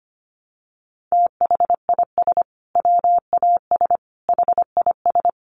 Morse code, heard as a keyed tone, about 25 wpm, 700 Hz. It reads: THIS WAS HIS
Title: Sentences from Top 1200 Words 25wpm